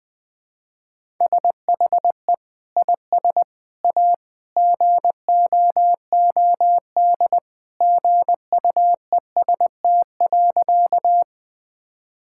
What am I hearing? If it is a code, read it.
Morse code, heard as a keyed tone, about 20 wpm, 700 Hz. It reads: SHE IS A GOOD GUEST.
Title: Sentences from Top 1600 Words 20wpm